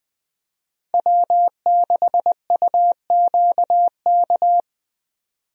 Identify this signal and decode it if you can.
Morse code, heard as a keyed tone, about 20 wpm, 700 Hz. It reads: W6UQK
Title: USA Call Signs - 5 Characters - 20wpm